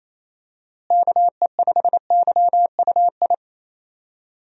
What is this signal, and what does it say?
Morse code, heard as a keyed tone, about 27 wpm, 700 Hz. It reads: KE5YUI
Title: USA Call Signs - 6 Characters - Repeated 2x - 28wpm